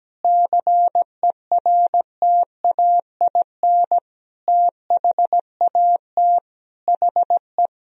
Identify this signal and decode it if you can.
Morse code, heard as a keyed tone, about 17 wpm, 700 Hz. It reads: CERTAIN THAT HE
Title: Sentences from Top 1000 Words - Review Part 2 - 17wpm